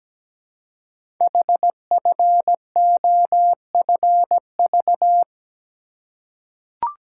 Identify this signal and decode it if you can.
Morse code, heard as a keyed tone, about 17 wpm, 700 Hz. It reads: HFOFV
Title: Code Groups - 5 Letters - 17wpm